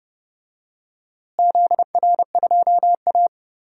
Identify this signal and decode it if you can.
Morse code, heard as a keyed tone, about 30 wpm, 700 Hz. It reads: ZR2A
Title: DXCC-100 Call Signs - 4 Characters - Repeated 2x - 30wpm